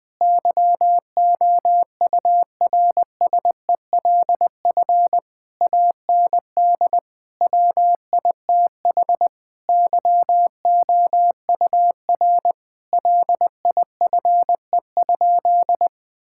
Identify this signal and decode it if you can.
Morse code, heard as a keyed tone, about 20 wpm, 700 Hz. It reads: YOURSELF AND WITH YOUR LIFE?
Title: Sentences from Top 1000 Words - Review Part 6 - 20wpm